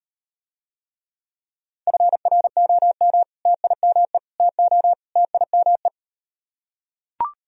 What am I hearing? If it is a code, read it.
Morse code, heard as a keyed tone, about 38 wpm, 700 Hz. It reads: FROM TIME TO TIME